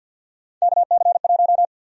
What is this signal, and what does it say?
Morse code, heard as a keyed tone, about 48 wpm, 700 Hz. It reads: KC1
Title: USA Call Sign Prefixes - Repeated 2x - 50wpm